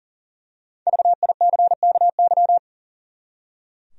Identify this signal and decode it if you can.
Morse code, heard as a keyed tone, about 40 wpm, 700 Hz. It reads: VICKY
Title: USA Common Names - 40wpm